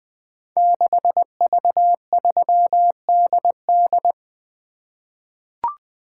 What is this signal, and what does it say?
Morse code, heard as a keyed tone, about 20 wpm, 700 Hz. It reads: 6V3DD
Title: Code Groups - 5 Letter:Numbers - 20wpm